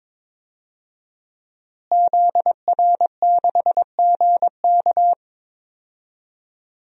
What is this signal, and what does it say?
Morse code, heard as a keyed tone, about 22 wpm, 700 Hz. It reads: ZR6GK
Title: DXCC-100 Call Signs - 5 Characters - Repeated 3x - 22wpm